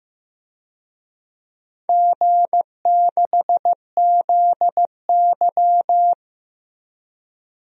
Morse code, heard as a keyed tone, about 15 wpm, 700 Hz. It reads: G6ZY
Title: DXCC-100 Call Signs - 4 Characters - Repeated 2x - 15wpm